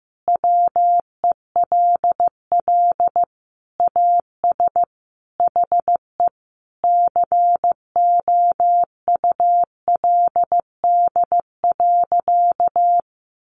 Morse code, heard as a keyed tone, about 14 wpm, 700 Hz. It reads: WELL AS HE COULD.
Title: Sentences from Top 500 Words 15wpm